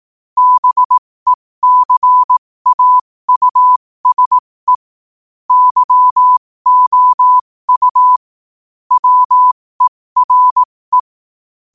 Morse code, heard as a keyed tone, about 18 wpm, spaced as for 15 wpm, 1000 Hz. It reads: BECAUSE YOU WERE
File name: qotdfiles / 15wpm